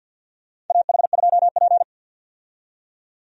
Morse code, heard as a keyed tone, about 51 wpm, 700 Hz. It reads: AH2P
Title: USA Call Signs - 4 Characters - 50wpm